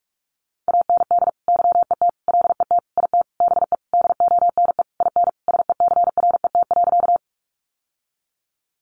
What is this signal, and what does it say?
Morse code, heard as a keyed tone, about 44 wpm, 700 Hz. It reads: AND YET LET IT BE DONE IN SECRET.
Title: Sentences from Top 1200 Words 45wpm